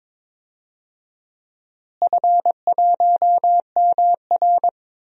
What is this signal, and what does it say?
Morse code, heard as a keyed tone, about 22 wpm, 700 Hz. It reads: F1MR
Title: DXCC-100 Call Signs - 4 Characters - Repeated 2x - 22wpm